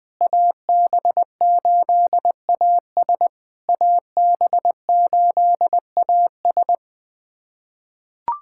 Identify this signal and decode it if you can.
Morse code, heard as a keyed tone, about 20 wpm, 700 Hz. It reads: AB8AS AB8AS
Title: POTA QSO Practice - 20wpm